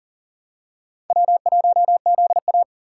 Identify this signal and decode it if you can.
Morse code, heard as a keyed tone, about 40 wpm, 700 Hz. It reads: W1ZA
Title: USA Call Signs - 3x Character Spacing - 40wpm